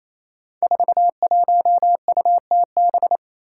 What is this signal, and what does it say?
Morse code, heard as a keyed tone, about 28 wpm, 700 Hz. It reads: 41UTB
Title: Five Challenging Letter-Number Combinations 28wpm